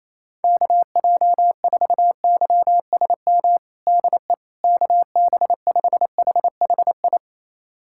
Morse code, heard as a keyed tone, about 28 wpm, 700 Hz. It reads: KJ4YSM DE KB5HHI
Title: USA Callsign DE Callsign - 6 Characters - 28wpm